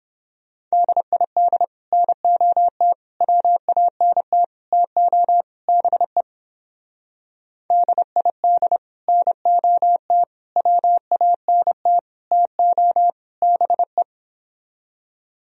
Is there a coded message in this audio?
Yes: Morse code from 0.7 to 14.0 s, heard as a keyed tone, about 29 wpm, 700 Hz. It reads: DID NOT WANT TO BE DID NOT WANT TO BE